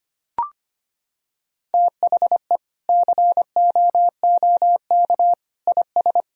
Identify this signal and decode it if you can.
Morse code, heard as a keyed tone, about 25 wpm, 700 Hz. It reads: THE COOK IS
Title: Sentences from Top 1000 Words - Review Part 4 - 25wpm